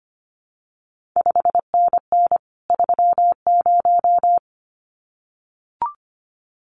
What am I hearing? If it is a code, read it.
Morse code, heard as a keyed tone, about 25 wpm, 700 Hz. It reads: 5NN 30
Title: CQ WW DX Contest Exchange Encore - 25wpm